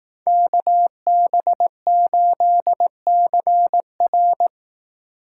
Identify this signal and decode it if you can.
Morse code, heard as a keyed tone, about 18 wpm, 700 Hz. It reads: KB8CR
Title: Speed Racing - USA 2x2 Call Signs 15wpm